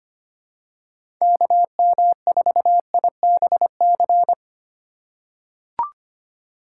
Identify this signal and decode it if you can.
Morse code, heard as a keyed tone, about 25 wpm, 700 Hz. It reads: KM4IBC
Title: USA Call Signs - 6 Characters - 3x Character Spacing - 25wpm